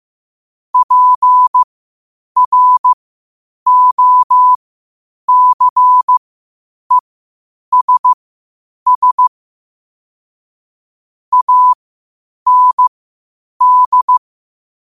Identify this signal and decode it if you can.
Morse code, heard as a keyed tone, about 15 wpm, spaced as for 8 wpm, 1000 Hz. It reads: PROCESS AND
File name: qotdfiles / 8wpm